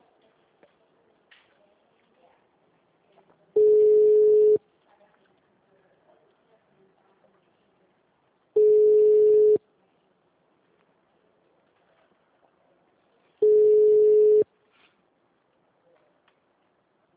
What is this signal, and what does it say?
Ringback tone (Europe, 425 Hz).